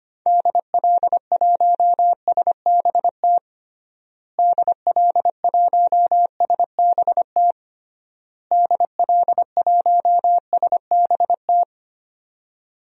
Morse code, heard as a keyed tone, about 25 wpm, 700 Hz. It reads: DL1SBT DL1SBT DL1SBT
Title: DXCC-100 Call Signs - 6 Characters - Repeated 3x - 25wpm